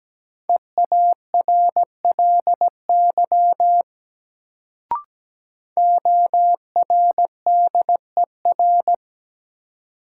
Morse code, heard as a keyed tone, about 17 wpm, 700 Hz. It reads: EARLY ORDER
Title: Sentences from Top 400 Words - New Word Drill - 17wpm